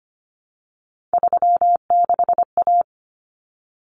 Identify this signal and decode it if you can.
Morse code, heard as a keyed tone, about 25 wpm, 700 Hz. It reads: 36A
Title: Common Cut Numbers - 3-Digit 25wpm